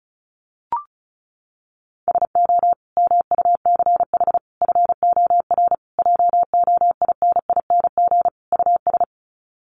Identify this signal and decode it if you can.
Morse code, heard as a keyed tone, about 34 wpm, 700 Hz. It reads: SO MUCH FOR JOINING US
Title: Sets of 5 Words 35wpm